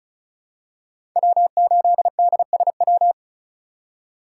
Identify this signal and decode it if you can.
Morse code, heard as a keyed tone, about 34 wpm, 700 Hz. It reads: W8DSW
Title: USA Call Signs - 5 Characters - 4x Character Spacing - 35wpm